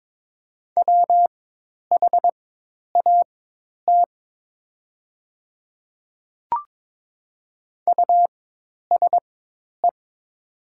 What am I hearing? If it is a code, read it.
Morse code, heard as a keyed tone, about 22 wpm, spaced as for 10 wpm, 700 Hz. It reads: WHAT USE
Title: 4x Character Spacing - Top 100 Words - 22wpm